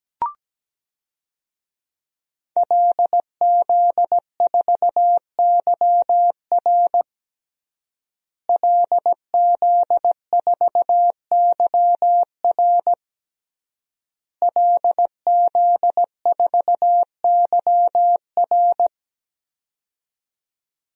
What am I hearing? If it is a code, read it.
Morse code, heard as a keyed tone, about 17 wpm, 700 Hz. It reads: LZ4YR LZ4YR LZ4YR
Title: DXCC-100 Call Signs - 5 Characters - Repeated 3x - 17wpm